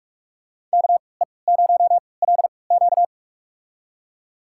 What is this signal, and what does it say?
Morse code, heard as a keyed tone, about 44 wpm, spaced as for 25 wpm, 700 Hz. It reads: KE0LQ
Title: USA Call Signs - 5 Characters - 3x Character Spacing - 45wpm